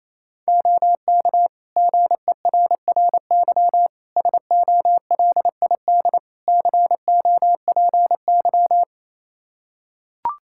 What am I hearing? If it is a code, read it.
Morse code, heard as a keyed tone, about 27 wpm, 700 Hz. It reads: OK GERRY SOLID COPY
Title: Practice QSO - Exchange 3 - More Info - 28wpm